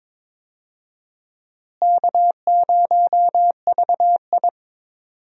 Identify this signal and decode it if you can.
Morse code, heard as a keyed tone, about 22 wpm, 700 Hz. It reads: K0VI
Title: USA Call Signs - 4 Characters - Repeated 3x - 22wpm